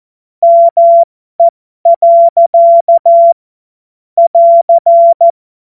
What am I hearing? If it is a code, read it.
Morse code, heard as a keyed tone, about 14 wpm, spaced as for 12 wpm, 680 Hz. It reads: ME. <AR>
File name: Morsekurs / Witze